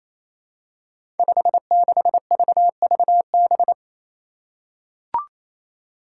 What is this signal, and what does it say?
Morse code, heard as a keyed tone, about 28 wpm, 700 Hz. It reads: 56VVB
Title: Five Challenging Letter-Number Combinations 28wpm